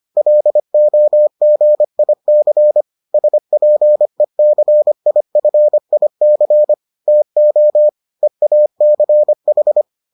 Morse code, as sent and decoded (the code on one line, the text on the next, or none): .-.. --- --. .. -.-. / ... .--. -.-. .. ..-. .. -.-. / - --- / . .- -.-. ....
LOGIC SPECIFIC TO EACH